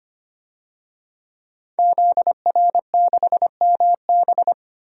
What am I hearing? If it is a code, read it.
Morse code, heard as a keyed tone, about 25 wpm, 700 Hz. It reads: ZR6MB
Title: DXCC-100 Call Signs - 5 Characters - 25wpm